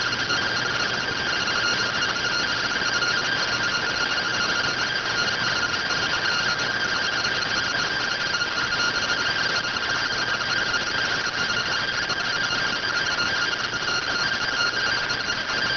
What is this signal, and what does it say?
Two-tone teleprinter signal, about 170 Hz shift.